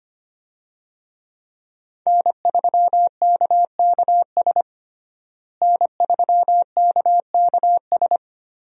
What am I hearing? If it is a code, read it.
Morse code, heard as a keyed tone, about 25 wpm, 700 Hz. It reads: N3KKS N3KKS
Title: USA Call Signs - 5 Characters - Repeated 3x - 25wpm